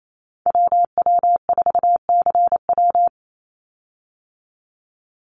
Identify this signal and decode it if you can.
Morse code, heard as a keyed tone, about 28 wpm, 700 Hz. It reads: WW4CW
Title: USA Call Signs 28wpm